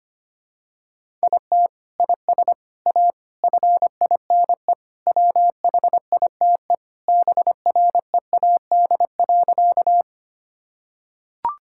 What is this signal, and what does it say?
Morse code, heard as a keyed tone, about 25 wpm, 700 Hz. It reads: IT IS A FINE WHITE BREAD.
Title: Sentences from Top 1600 Words 25wpm